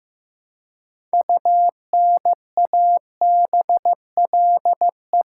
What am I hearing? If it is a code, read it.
Morse code, heard as a keyed tone, about 15 wpm, 700 Hz. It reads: UNABLE
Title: Six Letter Words - 15wpm